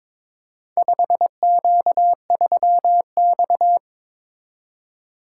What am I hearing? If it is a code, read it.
Morse code, heard as a keyed tone, about 22 wpm, 700 Hz. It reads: 5Q3X
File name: practice / 22wpm / Four Letter-Number Combinations - 22wpm